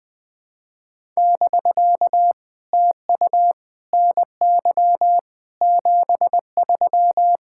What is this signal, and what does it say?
Morse code, heard as a keyed tone, about 20 wpm, 700 Hz. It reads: <BK> TU NY 73
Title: POTA QSO Practice - 20wpm